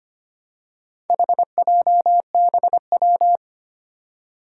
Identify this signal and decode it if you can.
Morse code, heard as a keyed tone, about 25 wpm, 700 Hz. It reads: HJBW
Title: Four Challenging Letter-Number Combinations 25wpm